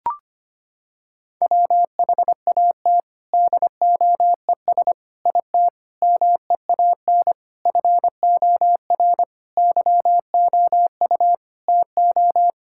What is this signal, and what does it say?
Morse code, heard as a keyed tone, about 25 wpm, 700 Hz. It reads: WHAT DOES IT MEAN FOR YOU TO
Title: Sentences from Top 1000 Words - Review Part 6 - 25wpm